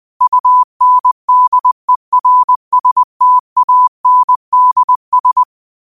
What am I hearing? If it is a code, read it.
Morse code, heard as a keyed tone, about 20 wpm, 1000 Hz. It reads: UNDERSTANDS
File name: qotdfiles / 20wpm